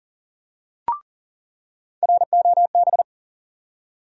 Morse code, heard as a keyed tone, about 40 wpm, 700 Hz. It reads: ROB
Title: USA Call Sign Postfixes - 3 Letters 40wpm